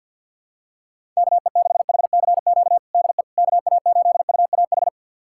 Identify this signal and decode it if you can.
Morse code, heard as a keyed tone, about 47 wpm, 700 Hz. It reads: KE6HCQ DE KA8UAH
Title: USA Callsign DE Callsign 50wpm